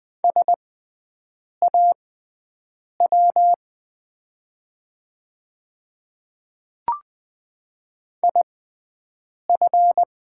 Morse code, heard as a keyed tone, about 20 wpm, spaced as for 7 wpm, 700 Hz. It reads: SAW IF
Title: 6x Character Spacing - Top 300 Words - 20wpm